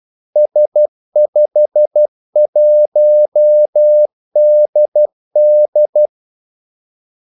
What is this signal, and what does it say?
Morse code, heard as a keyed tone, about 12 wpm, 600 Hz. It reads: S51DD
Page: 2 ekstra ordmellemrum mellem kaldesignalerne